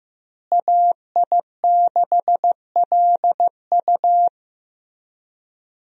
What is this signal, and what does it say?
Morse code, heard as a keyed tone, about 16 wpm, 700 Hz. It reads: AI6LU
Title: USA Call Signs - 5 Characters - Repeated 3x - 15wpm